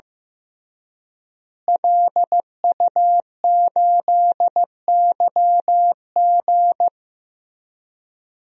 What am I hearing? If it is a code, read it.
Morse code, heard as a keyed tone, about 15 wpm, 700 Hz. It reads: LU8YG
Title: DXCC-100 Call Signs - 5 Characters - Repeated 3x - 15wpm